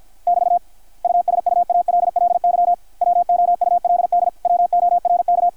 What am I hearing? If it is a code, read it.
Morse code, heard as a keyed tone, about 53 wpm, 720 Hz, with radio noise. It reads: <BT> USUALLY WOULD WORK
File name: archived / CRSnet